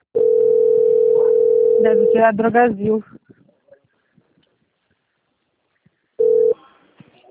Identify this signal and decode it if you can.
Ringback tone (North America), 440 + 480 Hz, with speech.
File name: out- - - - - .18890